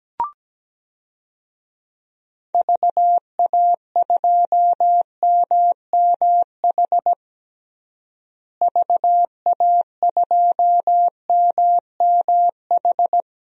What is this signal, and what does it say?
Morse code, heard as a keyed tone, about 17 wpm, 700 Hz. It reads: VA2MMH VA2MMH
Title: DXCC-100 Call Signs - 6 Characters - Repeated 2x - 17wpm